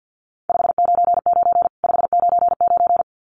Morse code, heard as a keyed tone, about 49 wpm, 700 Hz. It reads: 599 599
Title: Common QSO Phrases 50wpm